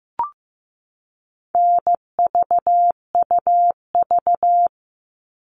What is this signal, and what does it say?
Morse code, heard as a keyed tone, about 15 wpm, 700 Hz. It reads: NVUV
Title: All Cut Numbers - 4-Digit 15wpm